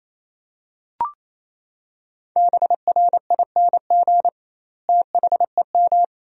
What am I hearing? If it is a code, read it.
Morse code, heard as a keyed tone, about 27 wpm, 700 Hz. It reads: BRING THEM
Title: 2x Word-Spacing - Sentences from Top 500 Words - 28wpm